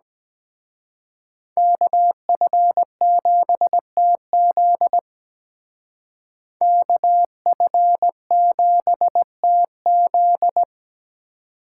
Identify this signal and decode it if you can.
Morse code, heard as a keyed tone, about 20 wpm, 700 Hz. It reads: KF7TZ KF7TZ
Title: Speed Racing - USA 2x2 Call Signs 17wpm